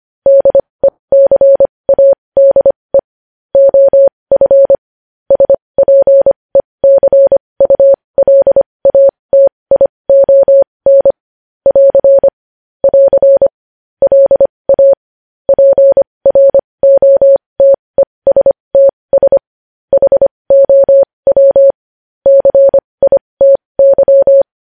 Morse code, heard as a keyed tone, about 25 wpm, spaced as for 20 wpm, 550 Hz. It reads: DECADE OF SPECULATION <AR> <AR> LA PROTESTS HOW CITY
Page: news-headlines-in-morse-code-at-20-wpm